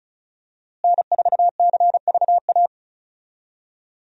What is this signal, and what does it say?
Morse code, heard as a keyed tone, about 36 wpm, 700 Hz. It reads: N4CVA
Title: USA Call Signs - 5 Characters - Repeated 3x - 35wpm